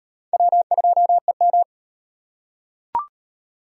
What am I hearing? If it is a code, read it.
Morse code, heard as a keyed tone, about 38 wpm, 700 Hz. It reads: W2EM